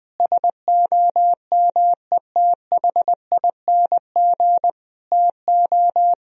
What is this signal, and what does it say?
Morse code, heard as a keyed tone, about 20 wpm, 700 Hz. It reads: SOMETHING TO
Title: Sentences from Top 1000 Words - Review Part 2 - 20wpm